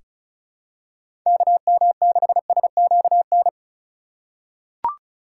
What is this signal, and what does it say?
Morse code, heard as a keyed tone, about 36 wpm, 700 Hz. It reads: KM6SQN
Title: USA Call Signs - 6 Characters - 2x Character Spacing - 35wpm